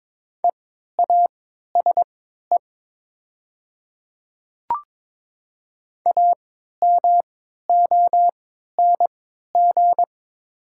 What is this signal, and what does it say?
Morse code, heard as a keyed tone, about 22 wpm, spaced as for 13 wpm, 700 Hz. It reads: EASE AMONG
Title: 3x Character Spacing - Top 500 Words - 22wpm